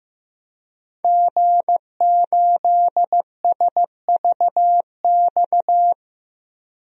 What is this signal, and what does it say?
Morse code, heard as a keyed tone, about 15 wpm, 700 Hz. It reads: G8SVX
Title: DXCC-100 Call Signs - 5 Characters - 15wpm